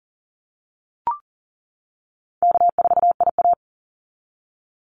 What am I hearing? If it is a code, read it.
Morse code, heard as a keyed tone, about 40 wpm, 700 Hz. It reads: K4IA
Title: Easy Call Signs - 40wpm